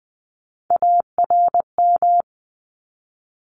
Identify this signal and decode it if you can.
Morse code, heard as a keyed tone, about 20 wpm, 700 Hz. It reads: ARM
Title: Top 1751-2000 Words 20wpm